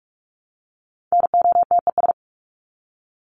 Morse code, heard as a keyed tone, about 44 wpm, 700 Hz. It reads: NOTES
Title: ES:S Suffix - Top 100 Words with Base Words 45wpm